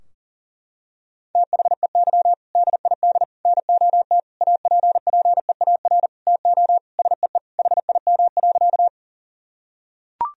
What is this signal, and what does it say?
Morse code, heard as a keyed tone, about 37 wpm, 700 Hz. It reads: THEY DID NOT APPEAR TO SEE HIM.
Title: Sentences from Top 1000 Words - Review Part 2 - 40wpm